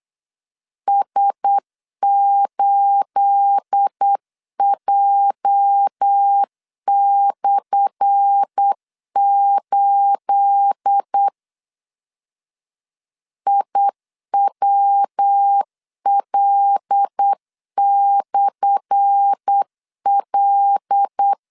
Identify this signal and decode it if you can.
Morse code, heard as a keyed tone, about 8 wpm, 800 Hz. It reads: S8J/8 IWL/L
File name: LEZIONI / 10 SIGNS / 10A SLASH 8wpm 3452